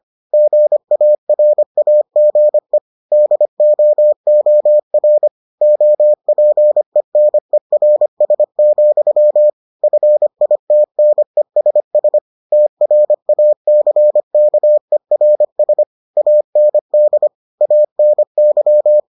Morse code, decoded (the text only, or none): GARAGE DOOR OPENERS, FITNESS TRACKERS AND ANY